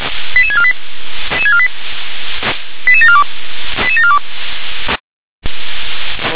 Five-tone selective call (ZVEI), heard as quick runs of five tones, tones of about 70 ms. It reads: 88648 88742 88742